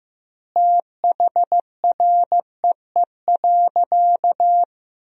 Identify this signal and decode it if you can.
Morse code, heard as a keyed tone, about 15 wpm, 700 Hz. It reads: THREE.
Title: 2x Word-Spacing - Sentences from Top 200 Words - 15wpm